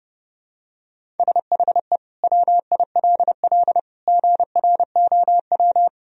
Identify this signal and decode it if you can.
Morse code, heard as a keyed tone, about 30 wpm, 700 Hz. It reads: SHE WILL GROW
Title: Sentences from Top 1000 Words - Review Part 1 - 30wpm